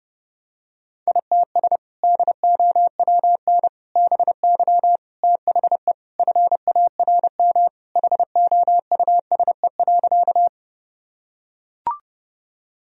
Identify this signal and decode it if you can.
Morse code, heard as a keyed tone, about 32 wpm, 700 Hz. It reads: ITS DOWN BY THE FARM HOUSE.
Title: Sentences from Top 1000 Words - Review Part 2 - 30wpm